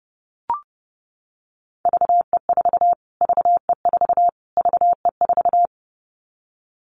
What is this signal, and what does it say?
Morse code, heard as a keyed tone, about 30 wpm, 690 Hz. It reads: VE4 VE4 VE4
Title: Callsign Prefixes for Top 25 DXCC - Repeated 3x - 30wpm